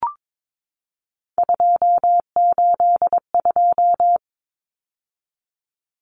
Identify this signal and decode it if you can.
Morse code, heard as a keyed tone, about 22 wpm, 700 Hz. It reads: 282
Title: Numbers - 3-Digit 22wpm